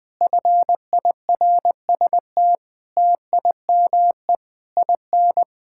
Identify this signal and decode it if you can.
Morse code, heard as a keyed tone, about 21 wpm, 700 Hz. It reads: FIRST TIME IN
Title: Sentences with 7-Word Phrases - 20wpm